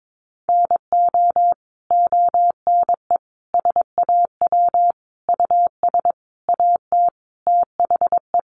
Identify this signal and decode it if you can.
Morse code, heard as a keyed tone, about 22 wpm, 700 Hz. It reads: NO ONE SAW US AT THE
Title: Sentences from Top 1200 Words 22wpm